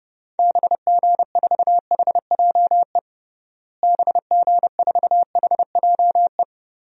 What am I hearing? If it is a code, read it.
Morse code, heard as a keyed tone, about 30 wpm, 700 Hz. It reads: BG4HJE BG4HJE
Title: DXCC-100 Call Signs - 6 Characters - Repeated 3x - 30wpm